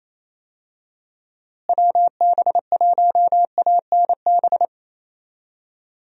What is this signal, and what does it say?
Morse code, heard as a keyed tone, about 27 wpm, 700 Hz. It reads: WB1ANB
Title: USA Call Signs - 6 Characters - Repeated 3x - 28wpm